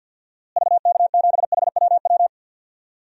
Morse code, heard as a keyed tone, about 50 wpm, 700 Hz. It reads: VK7HWW
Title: DXCC-100 Call Signs - 6 Characters - Repeated 3x - 50wpm